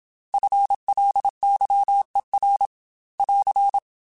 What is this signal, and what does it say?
Morse code, heard as a keyed tone, about 27 wpm, 800 Hz. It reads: FLYER <AR>